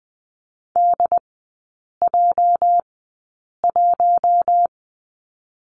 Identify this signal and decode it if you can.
Morse code, heard as a keyed tone, about 20 wpm, spaced as for 8 wpm, 690 Hz. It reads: DJ1
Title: 205 Lesson 33e - Letter j - Three Characters 20wpm